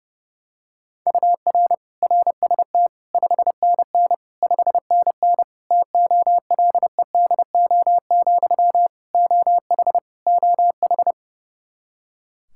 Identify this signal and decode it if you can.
Morse code, heard as a keyed tone, about 32 wpm, 700 Hz. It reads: UR RST 5NN 5NN TOLEDO, OH OH
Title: Practice QSO - Exchange 1 - Beginning - 30wpm